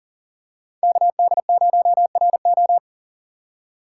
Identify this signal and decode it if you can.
Morse code, heard as a keyed tone, about 40 wpm, 700 Hz. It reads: KD0RO
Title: USA Call Signs - 3x Character Spacing - 40wpm